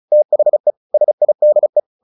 Morse code, heard as a keyed tone, about 34 wpm, 600 Hz. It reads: THE SIDE
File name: the old man and the sea 2 35wpm